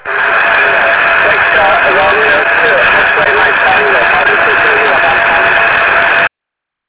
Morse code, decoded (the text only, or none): none